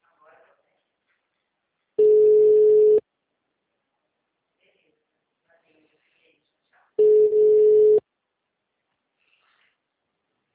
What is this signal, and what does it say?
Ringback tone (Europe, 425 Hz), with speech.